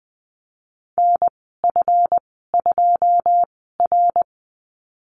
Morse code, heard as a keyed tone, about 20 wpm, spaced as for 15 wpm, 700 Hz. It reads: NF2R
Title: 147 Lesson 24g - Letter y - Call Signs 20wpm